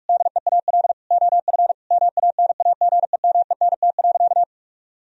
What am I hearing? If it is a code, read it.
Morse code, heard as a keyed tone, about 44 wpm, 700 Hz. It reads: DEAL OF MANAGEMENT.
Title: Sentences from Top 1000 Words - Review Part 4 - 45wpm